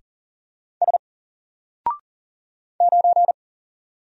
Morse code, heard as a keyed tone, about 40 wpm, 700 Hz. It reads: S 9
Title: Single Character (Letter:Number:Punctuation:Prosign) Alt 40wpm